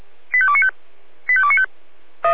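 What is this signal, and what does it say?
Five-tone selective call (ZVEI), heard as quick runs of five tones, tones of about 70 ms. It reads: 85295 85295
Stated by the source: ZVEI five-tone call 85220 received by SDR, from the Feuerwehr Orlamünde audio server.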